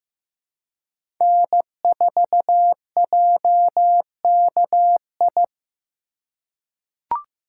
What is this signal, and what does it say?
Morse code, heard as a keyed tone, about 15 wpm, 700 Hz. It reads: N4JKI
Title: USA Call Signs - 5 Characters - 8x Character Spacing - 15wpm